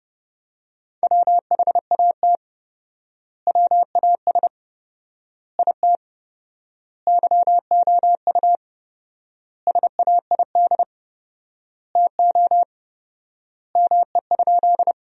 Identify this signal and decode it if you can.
Morse code, heard as a keyed tone, about 30 wpm, 700 Hz. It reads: WHAT WAS IT YOU SAID TO ME?
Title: 4x Word-Spacing - Sentences from Top 200 Words - 30wpm